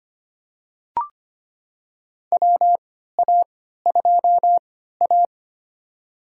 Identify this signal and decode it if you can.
Morse code, heard as a keyed tone, about 25 wpm, spaced as for 14 wpm, 700 Hz. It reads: WA2A
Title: USA Call Signs - 3x Character Spacing - 25wpm